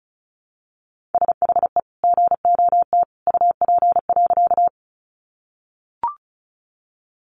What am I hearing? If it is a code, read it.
Morse code, heard as a keyed tone, about 34 wpm, 700 Hz. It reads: SHE GOT UP.
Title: Sentences from Top 300 Words 35wpm